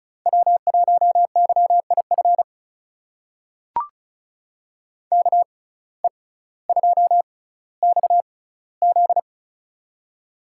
Morse code, heard as a keyed tone, about 36 wpm, spaced as for 18 wpm, 700 Hz. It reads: W1YIF KE2XZ
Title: USA Call Signs - 6x Character Spacing - 35wpm